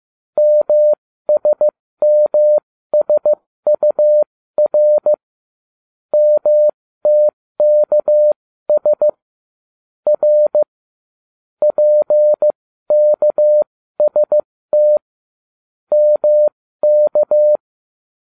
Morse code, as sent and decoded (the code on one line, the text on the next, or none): -- ... -- ... ..- .-. / -- - -.- ... / .-. / .--. -.- ... - / -- -.-
MSMSUR MTKS R PKST MK